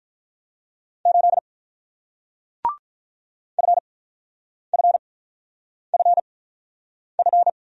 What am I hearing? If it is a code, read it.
Morse code, heard as a keyed tone, about 49 wpm, spaced as for 9 wpm, 700 Hz. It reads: 8 FFFF